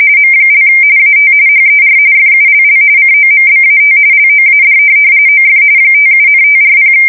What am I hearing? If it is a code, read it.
RTTY, about 45.45 baud. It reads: M) AT HER FULL COMBAT LOAD OF 57,540
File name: fsk170 / USS-Missouri